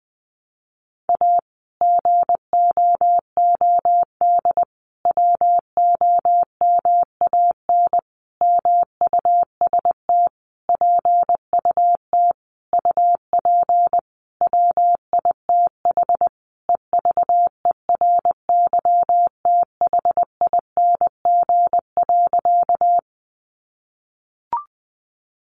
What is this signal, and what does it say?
Morse code, heard as a keyed tone, about 20 wpm, 700 Hz. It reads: A GOOD WOMAN MUST PUT UP WITH EVERYTHING.
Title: Sentences from Top 1100 Words 20wpm